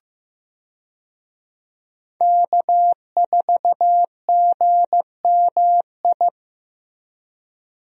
Morse code, heard as a keyed tone, about 15 wpm, 700 Hz. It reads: K4GMI
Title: USA Call Signs - 5 Characters - Repeated 2x - 15wpm